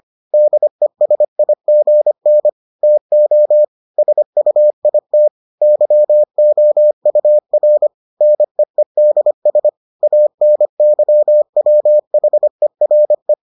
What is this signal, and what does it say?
Morse code, heard as a keyed tone, about 25 wpm, 600 Hz. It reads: DESIGN TO SUIT YOUR NEEDS ANYWHERE